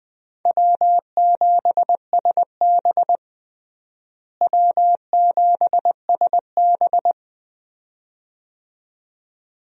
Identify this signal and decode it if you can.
Morse code, heard as a keyed tone, about 20 wpm, 700 Hz. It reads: W7SB W7SB
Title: USA Call Signs - 4 Characters - Repeated 2x - 20wpm